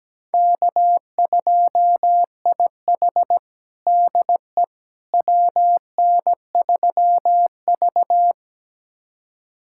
Morse code, heard as a keyed tone, about 17 wpm, 700 Hz. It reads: K2IH DE WN3V
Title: USA Callsign DE Callsign - 4 Characters - 17wpm